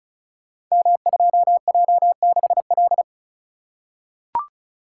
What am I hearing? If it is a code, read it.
Morse code, heard as a keyed tone, about 36 wpm, 700 Hz. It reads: M2J6L
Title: Code Groups - 5 Letter:Numbers - 35wpm